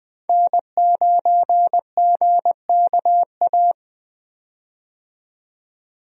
Morse code, heard as a keyed tone, about 20 wpm, 700 Hz. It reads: N9GKA
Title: USA Call Signs - 5 Characters - 20wpm